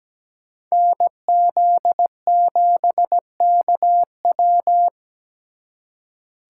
Morse code, heard as a keyed tone, about 20 wpm, 700 Hz. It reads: NZ7KW
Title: Speed Racing - USA 2x2 Call Signs 17wpm